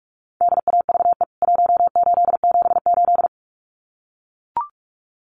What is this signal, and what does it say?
Morse code, heard as a keyed tone, about 44 wpm, 700 Hz. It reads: DAVE 1878
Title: CWT Exchange 45wpm